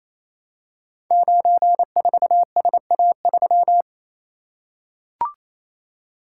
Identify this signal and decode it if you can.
Morse code, heard as a keyed tone, about 27 wpm, 700 Hz. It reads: 94SA3
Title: Code Groups - 5 Letter:Numbers - 28wpm